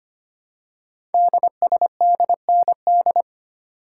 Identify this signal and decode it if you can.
Morse code, heard as a keyed tone, about 25 wpm, 700 Hz. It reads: DSDND
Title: Code Groups - 5 Letters - 25wpm